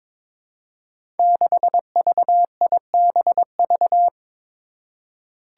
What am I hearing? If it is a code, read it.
Morse code, heard as a keyed tone, about 22 wpm, 700 Hz. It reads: 6VIBV
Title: Five Challenging Letter-Number Combinations 22wpm